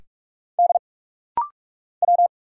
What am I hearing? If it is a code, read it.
Morse code, heard as a keyed tone, about 46 wpm, 700 Hz. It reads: D W
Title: Single Letters - Rapid-Fire 45wpm